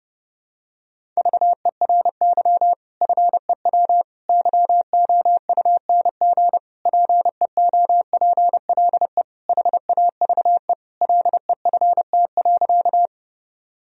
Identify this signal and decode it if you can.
Morse code, heard as a keyed tone, about 32 wpm, 700 Hz. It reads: VERY FEW YOUNG PEOPLE HAVE LEFT.
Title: Sentences from Top 1000 Words - Review Part 1 - 30wpm